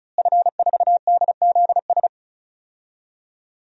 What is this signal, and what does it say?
Morse code, heard as a keyed tone, about 34 wpm, 700 Hz. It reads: F4DZS
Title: DXCC-100 Call Signs - 5 Characters - Repeated 3x - 35wpm